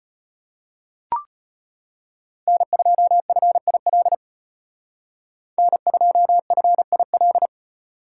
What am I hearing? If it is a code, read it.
Morse code, heard as a keyed tone, about 33 wpm, 700 Hz. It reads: N2FIL N2FIL